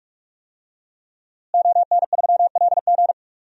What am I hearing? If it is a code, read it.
Morse code, heard as a keyed tone, about 46 wpm, 700 Hz. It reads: ON3LG
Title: DXCC-100 Call Signs - 5 Characters - Repeated 2x - 45wpm